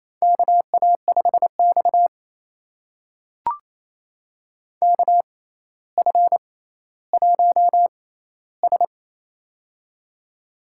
Morse code, heard as a keyed tone, about 28 wpm, spaced as for 14 wpm, 700 Hz. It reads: KA5X KF1S
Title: USA Call Signs - 4 Characters - 6x Character Spacing - 28wpm